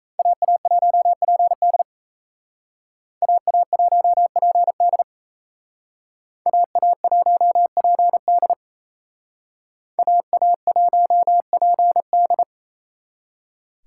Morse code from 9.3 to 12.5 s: AA1PD

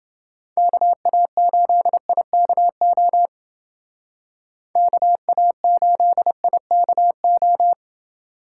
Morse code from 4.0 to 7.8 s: KA8IKO

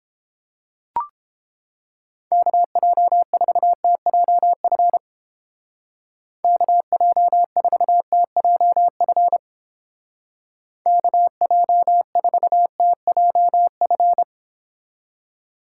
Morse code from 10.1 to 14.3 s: KJ4TJF

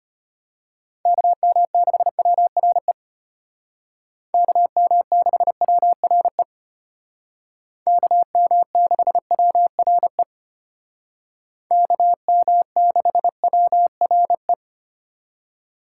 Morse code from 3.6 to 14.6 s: KM6WRE KM6WRE KM6WRE

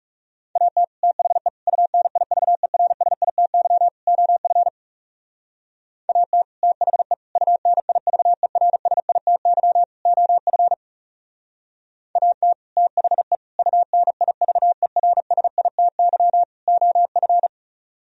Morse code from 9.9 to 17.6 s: OF AT THE UNIVERSITY OF

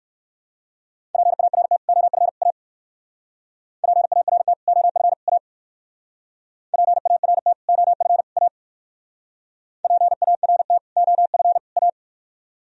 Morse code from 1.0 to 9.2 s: PART OF A PART OF A PART OF A

